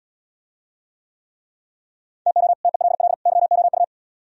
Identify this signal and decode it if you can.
Morse code, heard as a keyed tone, about 76 wpm, 700 Hz. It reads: TO TELL YOU